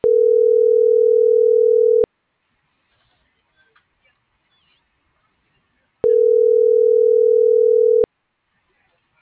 Ringback tone (North America), 440 + 480 Hz.